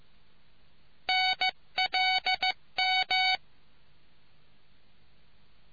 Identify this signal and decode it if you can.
Morse code, heard as a keyed tone, about 15 wpm, 750 Hz. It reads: NLM